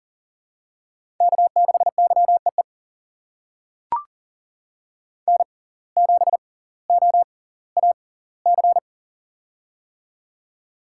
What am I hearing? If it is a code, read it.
Morse code, heard as a keyed tone, about 37 wpm, spaced as for 20 wpm, 700 Hz. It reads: K6YEE N7OAC
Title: USA Call Signs - 5 Characters - 6x Character Spacing - 40wpm